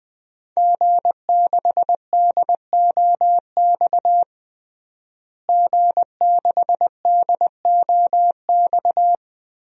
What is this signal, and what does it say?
Morse code, heard as a keyed tone, about 20 wpm, 700 Hz. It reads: G6DOX G6DOX
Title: DXCC-100 Call Signs - 5 Characters - Repeated 3x - 20wpm